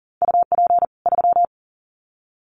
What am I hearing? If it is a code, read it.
Morse code, heard as a keyed tone, about 40 wpm, 700 Hz. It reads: UP 3
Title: Common QSO Phrases 40wpm